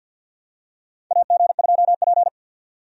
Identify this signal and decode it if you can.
Morse code, heard as a keyed tone, about 52 wpm, 700 Hz. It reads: AG2P
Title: USA Call Signs - 4 Characters - 50wpm